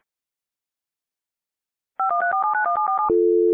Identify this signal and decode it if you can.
DTMF, heard as telephone touch tones, tones of about 120 ms. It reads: 5134791747
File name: radiocap / REC z